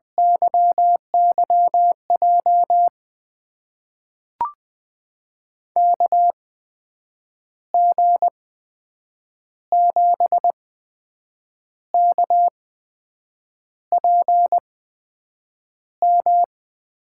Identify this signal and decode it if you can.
Morse code, heard as a keyed tone, about 20 wpm, spaced as for 7 wpm, 700 Hz. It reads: YYJ KG7KPM
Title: USA Call Signs - 6 Characters - 8x Character Spacing - 20wpm